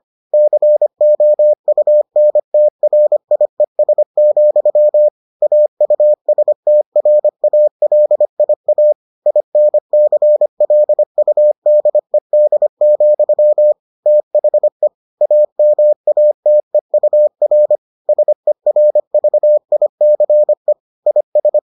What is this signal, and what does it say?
Morse code, heard as a keyed tone, about 25 wpm, 600 Hz. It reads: COUNTRIES, AUSTRALIA INCLUDED, THE AMATEUR SERVICE IS